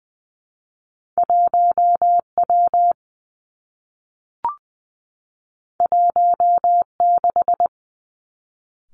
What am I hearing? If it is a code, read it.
Morse code, heard as a keyed tone, about 20 wpm, 700 Hz. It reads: 1 W 1 6
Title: Two Challenging Letter-Number Combinations 20wpm